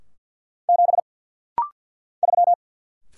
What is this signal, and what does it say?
Morse code, heard as a keyed tone, about 50 wpm, 700 Hz. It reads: 7 3
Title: Single Letter-Number Alt - Rapid-Fire 50wpm